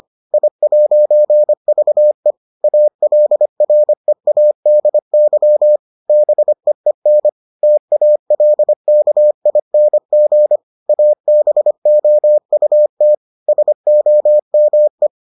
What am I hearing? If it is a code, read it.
Morse code, heard as a keyed tone, about 25 wpm, 600 Hz. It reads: I'VE ALREADY BEEN TALKING ABOUT SOME